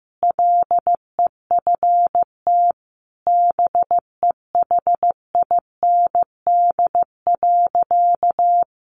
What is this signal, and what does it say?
Morse code, heard as a keyed tone, about 14 wpm, 700 Hz. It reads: LEFT BEHIND.
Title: A Sound Of Thunder 15wpm